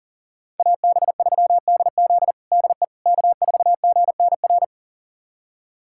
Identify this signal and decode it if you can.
Morse code, heard as a keyed tone, about 40 wpm, 700 Hz. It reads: AB3DZ DE K4GNR
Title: USA Callsign DE Callsign - 5 Characters - 40wpm